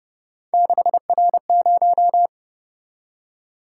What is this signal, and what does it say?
Morse code, heard as a keyed tone, about 29 wpm, 700 Hz. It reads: 6R0